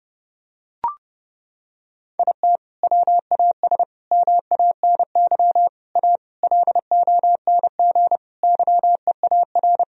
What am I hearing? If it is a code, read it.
Morse code, heard as a keyed tone, about 30 wpm, 700 Hz. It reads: IT WAS MANY A LONG YEAR
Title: Sentences from Top 1000 Words - Review Part 1 - 30wpm